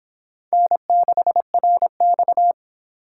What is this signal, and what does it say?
Morse code, heard as a keyed tone, about 26 wpm, 700 Hz. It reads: N6RX